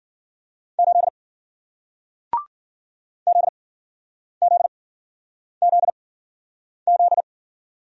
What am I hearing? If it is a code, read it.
Morse code, heard as a keyed tone, about 53 wpm, spaced as for 9 wpm, 700 Hz. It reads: 8 ZZZZ